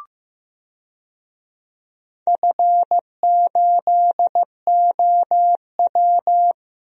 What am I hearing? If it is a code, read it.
Morse code, heard as a keyed tone, about 16 wpm, 700 Hz. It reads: F8OW
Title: DXCC-100 Call Signs - 4 Characters - Repeated 3x - 15wpm